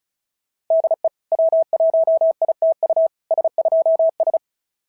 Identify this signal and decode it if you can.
Morse code, heard as a keyed tone, about 36 wpm, 690 Hz. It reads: DE W1ITU S2S
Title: SOTA QSO Practice 35wpm